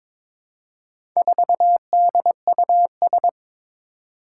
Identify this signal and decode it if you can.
Morse code, heard as a keyed tone, about 22 wpm, 700 Hz. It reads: 4DUS
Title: Four Challenging Letter-Number Combinations 22wpm